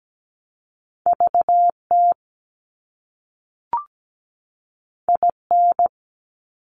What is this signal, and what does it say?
Morse code, heard as a keyed tone, about 17 wpm, 700 Hz. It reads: VT IN
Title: US States 17wpm